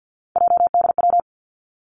Morse code, heard as a keyed tone, about 50 wpm, 700 Hz. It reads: JDW
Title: Three Challenging Letter-Number Combinations 50wpm